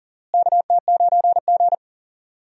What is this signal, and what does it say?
Morse code, heard as a keyed tone, about 38 wpm, 700 Hz. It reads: KT9G
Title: USA Call Signs - 4 Characters - Repeated 2x - 40wpm